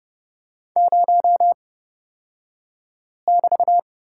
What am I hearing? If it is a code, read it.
Morse code, heard as a keyed tone, about 30 wpm, spaced as for 5 wpm, 700 Hz. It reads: 0<BT>
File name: Lesson 37d - Letter x - Two Characters 30wpm